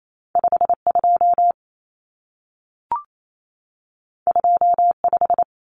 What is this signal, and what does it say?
Morse code, heard as a keyed tone, about 28 wpm, 700 Hz. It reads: 52 25
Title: Numbers - 2-Digit 28wpm